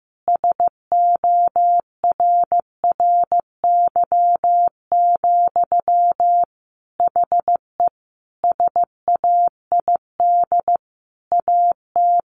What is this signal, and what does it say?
Morse code, heard as a keyed tone, about 15 wpm, 700 Hz. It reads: SORRY, HE SAID AT
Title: A Sound Of Thunder 15wpm